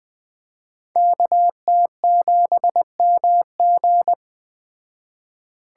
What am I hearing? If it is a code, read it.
Morse code, heard as a keyed tone, about 20 wpm, 700 Hz. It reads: KT7MG
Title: Speed Racing - USA 2x2 Call Signs 17wpm